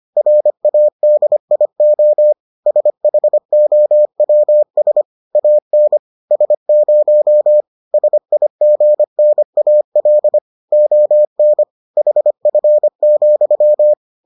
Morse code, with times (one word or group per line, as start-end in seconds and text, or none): 0.2-2.3 RADIO
2.7-5.0 SHOWS
5.3-6.0 AN
6.3-7.6 S0
7.9-10.4 SIGNAL
10.7-11.6 ON
12.0-13.9 HF,